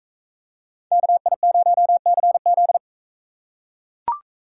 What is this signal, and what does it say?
Morse code, heard as a keyed tone, about 44 wpm, 700 Hz. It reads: KI0CZ